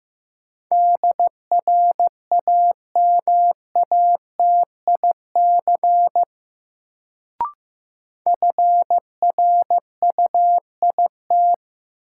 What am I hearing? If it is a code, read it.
Morse code, heard as a keyed tone, about 15 wpm, 700 Hz. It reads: DRAMATIC FRUIT
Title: Sentences from Top 1300 Words - New Word Drill Alt - 15wpm